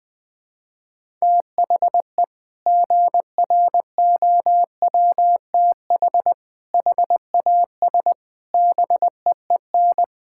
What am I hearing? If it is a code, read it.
Morse code, heard as a keyed tone, about 20 wpm, 700 Hz. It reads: THE GROWTH HAS BEEN
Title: Sentences from Top 1000 Words - Review Part 5 - 20wpm